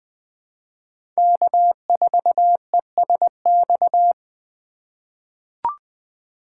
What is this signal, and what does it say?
Morse code, heard as a keyed tone, about 20 wpm, 700 Hz. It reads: K4ESX
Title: USA Call Signs - 3x Character Spacing - 20wpm